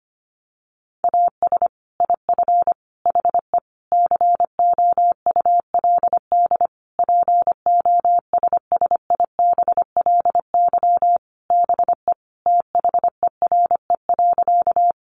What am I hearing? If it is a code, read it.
Morse code, heard as a keyed tone, about 25 wpm, 700 Hz. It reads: AS IF HE COULD POSSIBLY BE THERE.
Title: Sentences from Top 1300 Words 25wpm